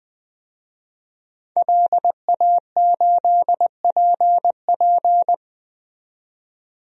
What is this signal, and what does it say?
Morse code, heard as a keyed tone, about 20 wpm, 700 Hz. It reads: LA8PP
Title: DXCC-100 Call Signs - 5 Characters - 20wpm